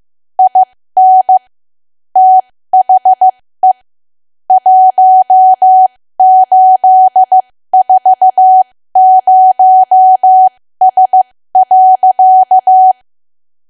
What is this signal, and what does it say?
Morse code, heard as a keyed tone, about 15 wpm, spaced as for 13 wpm, 750 Hz. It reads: IN THE 1840S.
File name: cwa / W1AW-Level2 / 13WPM